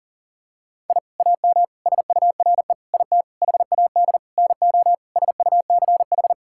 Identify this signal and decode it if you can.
Morse code, heard as a keyed tone, about 40 wpm, 700 Hz. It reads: I AM SURE IT HAD NO SUCH
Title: Sentences from Top 1000 Words - Review Part 6 - 40wpm